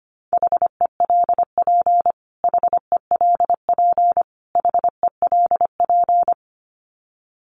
Morse code, heard as a keyed tone, about 25 wpm, 700 Hz. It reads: HELP HELP HELP
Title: Top 500 Words Repeated 25wpm